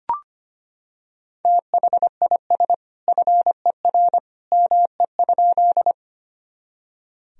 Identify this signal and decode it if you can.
Morse code, heard as a keyed tone, about 25 wpm, 700 Hz. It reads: THIS FER ME?
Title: Sentences from Top 200 Words and Abbreviations 25wpm